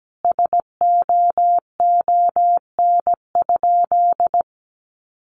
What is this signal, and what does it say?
Morse code, heard as a keyed tone, about 17 wpm, 700 Hz. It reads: SOON?
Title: Sentences from Top 1300 Words 17wpm